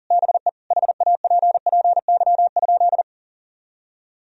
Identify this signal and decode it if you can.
Morse code, heard as a keyed tone, about 40 wpm, 700 Hz. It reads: BE HAPPY?
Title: Sentences from Top 1000 Words - Review Part 2 - 40wpm